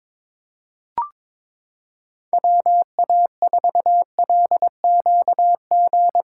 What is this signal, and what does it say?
Morse code, heard as a keyed tone, about 22 wpm, 700 Hz. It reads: WA4LQG
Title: USA Callsign DE Callsign - 6 Characters - 22wpm